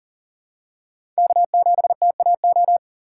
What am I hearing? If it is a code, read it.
Morse code, heard as a keyed tone, about 40 wpm, 700 Hz. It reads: K7TAO
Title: USA Call Signs - 5 Characters - 2x Character Spacing - 40wpm